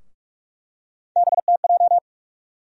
Morse code, heard as a keyed tone, about 46 wpm, 700 Hz. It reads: BTJ
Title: USA Call Sign Postfixes - 3 Letters 45wpm